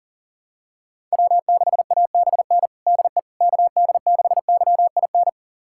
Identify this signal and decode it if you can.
Morse code, heard as a keyed tone, about 40 wpm, 700 Hz. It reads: W6ABN DE KD6YIN
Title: USA Callsign DE Callsign 40wpm